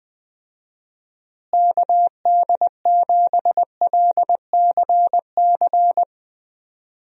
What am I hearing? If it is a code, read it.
Morse code, heard as a keyed tone, about 20 wpm, 700 Hz. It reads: KD7LCC
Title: USA Call Signs - 6 Characters - Repeated 2x - 20wpm